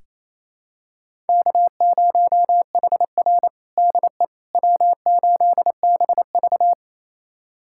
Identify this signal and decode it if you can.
Morse code, heard as a keyed tone, about 28 wpm, 700 Hz. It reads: K0HR DE W8BV
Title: USA Callsign DE Callsign - 4 Characters - 28wpm